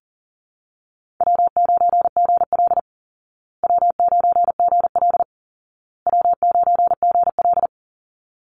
Morse code, heard as a keyed tone, about 40 wpm, 700 Hz. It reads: W9GL W9GL W9GL
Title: Easy Call Signs Repeated 3x - 40wpm